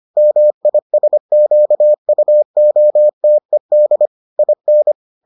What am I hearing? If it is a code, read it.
Morse code, heard as a keyed tone, about 25 wpm, 600 Hz. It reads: MISQUOTED IN